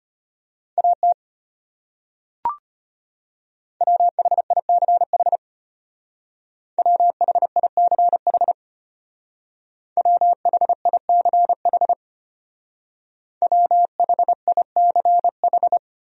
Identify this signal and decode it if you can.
Morse code, heard as a keyed tone, about 30 wpm, 700 Hz. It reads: AT WHICH WHICH WHICH WHICH
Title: Speed-Racing Top 300 Words 25wpm